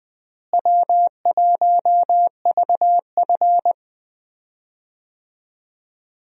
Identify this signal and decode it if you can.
Morse code, heard as a keyed tone, about 21 wpm, 700 Hz. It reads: W1VF
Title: USA Call Signs - 4 Characters - Repeated 3x - 20wpm